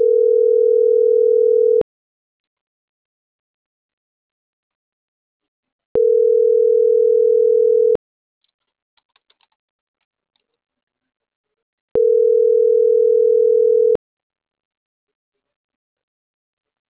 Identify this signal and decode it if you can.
Ringback tone (North America), 440 + 480 Hz.